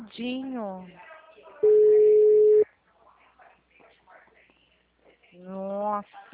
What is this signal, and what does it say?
Ringback tone (Europe, 425 Hz), with speech.